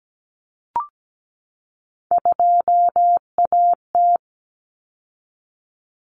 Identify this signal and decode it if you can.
Morse code, heard as a keyed tone, about 17 wpm, 700 Hz. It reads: 2AT
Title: Common Cut Numbers - 3-Digit 17wpm